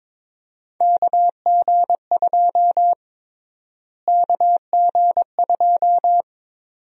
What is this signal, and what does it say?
Morse code, heard as a keyed tone, about 21 wpm, 700 Hz. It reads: KG2 KG2
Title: USA Call Sign Prefixes - Repeated 2x - 22wpm